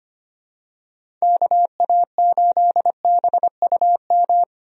Morse code, heard as a keyed tone, about 25 wpm, 700 Hz. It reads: KA8BUM
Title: USA Call Signs - 6 Characters - Repeated 3x - 25wpm